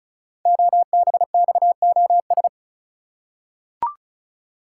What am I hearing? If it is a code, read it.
Morse code, heard as a keyed tone, about 36 wpm, 700 Hz. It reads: OBXOS
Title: Code Groups - 5 Letters - 35wpm